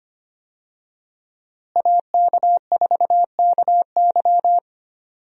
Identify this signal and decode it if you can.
Morse code, heard as a keyed tone, about 25 wpm, 700 Hz. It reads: AK4KY
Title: USA Call Signs - 5 Characters - Repeated 3x - 25wpm